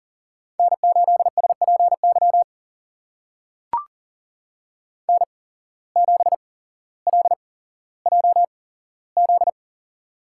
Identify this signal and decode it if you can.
Morse code, heard as a keyed tone, about 38 wpm, spaced as for 17 wpm, 700 Hz. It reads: N8SPY N7LJZ
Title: USA Call Signs - 5 Characters - 8x Character Spacing - 40wpm